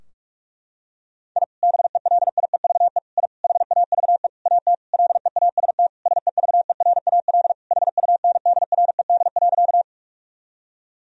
Morse code, heard as a keyed tone, about 47 wpm, 700 Hz. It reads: I BELIEVE I HAVE AT LEAST SEVERAL HUNDRED.
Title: Sentences from Top 1000 Words - Review Part 2 - 45wpm